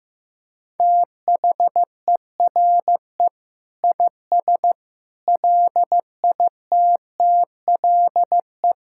Morse code, heard as a keyed tone, about 15 wpm, 700 Hz. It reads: THERE IS LITTLE
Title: Sentences from Top 1000 Words - Review Part 6 - 15wpm